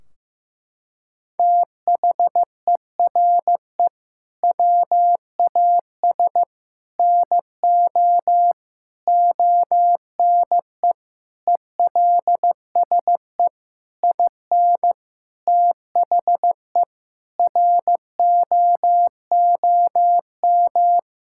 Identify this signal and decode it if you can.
Morse code, heard as a keyed tone, about 15 wpm, 700 Hz. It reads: THERE WAS NO ONE ELSE IN THE ROOM
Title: Sentences from Top 1600 Words 15wpm